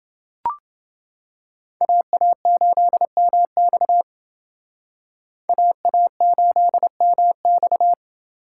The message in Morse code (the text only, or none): AA8MX AA8MX